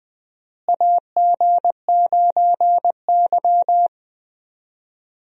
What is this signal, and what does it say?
Morse code, heard as a keyed tone, about 20 wpm, 700 Hz. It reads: AG9Y
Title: USA Call Signs - 4 Characters - Repeated 2x - 20wpm